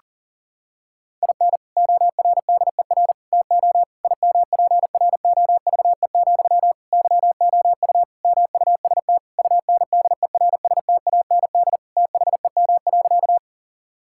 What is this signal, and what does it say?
Morse code, heard as a keyed tone, about 37 wpm, 700 Hz. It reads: IN ORDER TO IMPROVE, YOU MUST UNDERSTAND THEM.
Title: Sentences from Top 1000 Words - Review Part 5 - 40wpm